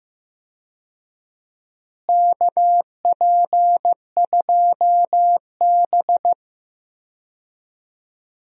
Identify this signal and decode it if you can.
Morse code, heard as a keyed tone, about 15 wpm, 700 Hz. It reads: KP2B
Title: DXCC-100 Call Signs - 4 Characters - Repeated 3x - 15wpm